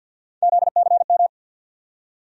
Morse code, heard as a keyed tone, about 47 wpm, 700 Hz. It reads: ZCM
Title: USA Call Sign Postfixes - 3 Letters 50wpm